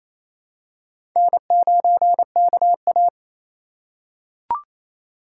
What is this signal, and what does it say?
Morse code, heard as a keyed tone, about 28 wpm, 700 Hz. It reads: N9KA
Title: USA Call Signs - 2x Character Spacing - 28wpm